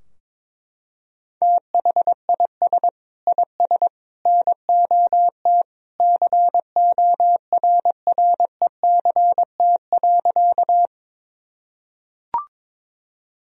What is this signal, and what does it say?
Morse code, heard as a keyed tone, about 22 wpm, 700 Hz. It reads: THIS IS NOT CORRECT.
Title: Sentences from Top 1000 Words - Review Part 2 - 22wpm